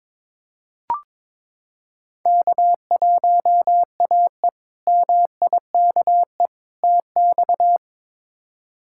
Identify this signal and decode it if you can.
Morse code, heard as a keyed tone, about 22 wpm, 700 Hz. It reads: K1AE MIKE TX
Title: K1USN SST QSO Practice - 22wpm